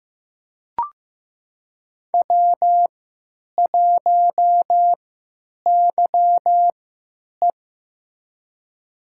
Morse code, heard as a keyed tone, about 15 wpm, spaced as for 8 wpm, 700 Hz. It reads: W1YE
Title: USA Call Signs - 4 Characters - 3x Character Spacing - 15wpm